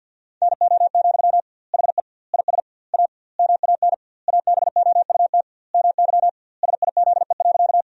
Morse code, heard as a keyed tone, about 52 wpm, 700 Hz. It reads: NO, HE IS A MAN ABOUT MY SIZE.
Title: 2x Word-Spacing - Sentences from Top 400 Words - 50wpm